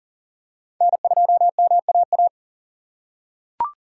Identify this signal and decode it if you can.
Morse code, heard as a keyed tone, about 40 wpm, 700 Hz. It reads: N2MAA
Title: USA Call Signs - 6x Character Spacing - 40wpm